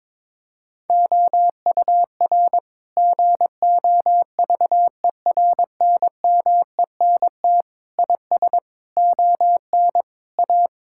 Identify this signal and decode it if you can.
Morse code, heard as a keyed tone, about 22 wpm, 700 Hz. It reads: OUR GOVERNMENT IS ON A
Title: Sentences from Top 1000 Words - Review Part 5 - 22wpm